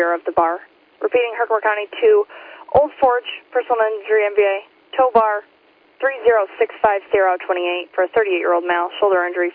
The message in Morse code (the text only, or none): none